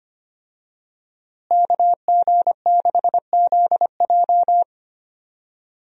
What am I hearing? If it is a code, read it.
Morse code, heard as a keyed tone, about 25 wpm, 700 Hz. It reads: KG6ZJ
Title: USA Call Signs - 5 Characters - Repeated 3x - 25wpm